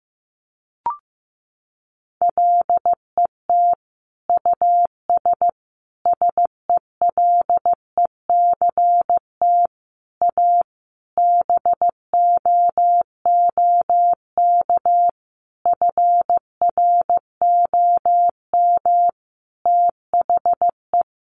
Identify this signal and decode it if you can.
Morse code, heard as a keyed tone, about 15 wpm, 700 Hz. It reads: LET US SELECT A BOOK FROM THE
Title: Sentences from Top 1300 Words 15wpm